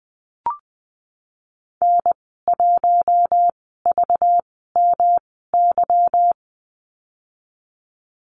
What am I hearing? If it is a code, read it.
Morse code, heard as a keyed tone, about 20 wpm, spaced as for 15 wpm, 700 Hz. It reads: N1VMY
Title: 200 Lesson 32g - Letter k - Call Signs 20wpm